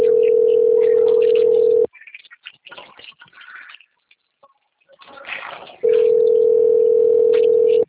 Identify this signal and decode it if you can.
Ringback tone (North America), 440 + 480 Hz.
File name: rg-1600-unknown- - - .53551